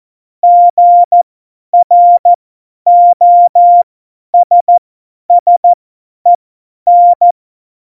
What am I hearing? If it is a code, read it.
Morse code, heard as a keyed tone, about 14 wpm, spaced as for 10 wpm, 700 Hz. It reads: GROSSEN
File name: Morsekurs / Witze / Witz 21 Juli 2020, 50 F70